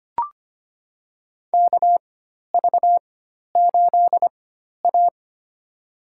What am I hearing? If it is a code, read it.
Morse code, heard as a keyed tone, about 25 wpm, spaced as for 12 wpm, 690 Hz. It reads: KV8A
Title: USA Call Signs - 4 Characters - 4x Character Spacing - 25wpm